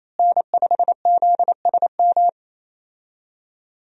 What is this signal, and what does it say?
Morse code, heard as a keyed tone, about 28 wpm, 700 Hz. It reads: N5ZSM
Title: USA Call Signs - 5 Characters - Repeated 3x - 28wpm